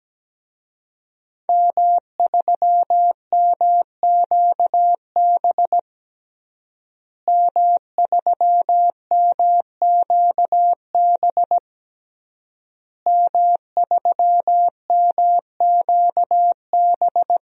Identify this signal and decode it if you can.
Morse code, heard as a keyed tone, about 17 wpm, 700 Hz. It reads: M3MQB M3MQB M3MQB
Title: DXCC-100 Call Signs - 5 Characters - Repeated 3x - 17wpm